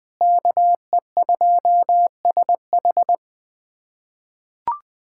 Morse code, heard as a keyed tone, about 21 wpm, 700 Hz. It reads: KE2SH
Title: USA Callsign DE Callsign - 5 Characters - 20wpm